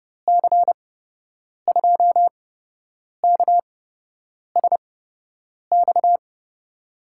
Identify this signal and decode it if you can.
Morse code, heard as a keyed tone, about 30 wpm, spaced as for 8 wpm, 700 Hz. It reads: C2KSX
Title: USA Call Signs - 6 Characters - 8x Character Spacing - 30wpm